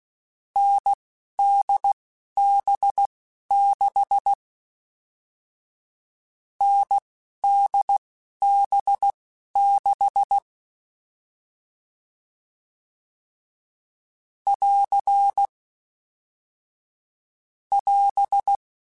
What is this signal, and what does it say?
Morse code, heard as a keyed tone, about 16 wpm, spaced as for 12 wpm, 800 Hz. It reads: NDB6 NDB6 <AR> <AS>
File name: Short Runs / Short Runs-mp3s / NDB NDB6 @10